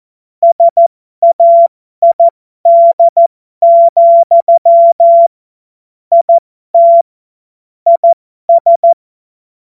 Morse code, heard as a keyed tone, about 14 wpm, spaced as for 12 wpm, 680 Hz. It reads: SAID, IT IS
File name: Morsekurs / Witze